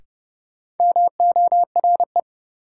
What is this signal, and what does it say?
Morse code, heard as a keyed tone, about 30 wpm, 700 Hz. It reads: MORE
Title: Sets of 2 Words - New Word Drill - 30wpm